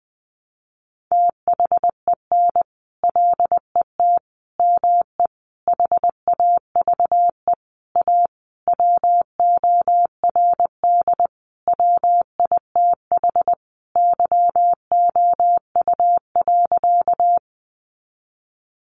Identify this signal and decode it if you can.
Morse code, heard as a keyed tone, about 20 wpm, 700 Hz. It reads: THEN LET ME HAVE A WORD WITH YOU.
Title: Sentences from Top 300 Words 20wpm